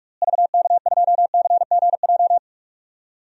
Morse code, heard as a keyed tone, about 41 wpm, 700 Hz. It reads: VK2CGJ
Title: DXCC-100 Call Signs - 6 Characters - Repeated 3x - 45wpm